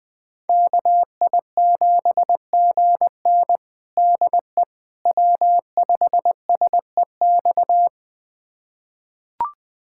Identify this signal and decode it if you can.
Morse code, heard as a keyed tone, about 20 wpm, 700 Hz. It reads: KI7GN DE W5SEX
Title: USA Callsign DE Callsign - 5 Characters - 20wpm